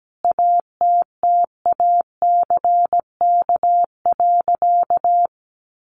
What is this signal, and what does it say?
Morse code, heard as a keyed tone, about 17 wpm, 700 Hz. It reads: ATTACK.
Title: Sentences from Top 1500 Words 17wpm